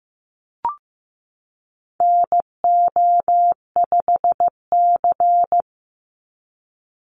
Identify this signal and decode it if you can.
Morse code, heard as a keyed tone, about 15 wpm, 700 Hz. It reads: NO5C
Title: USA Call Signs 15wpm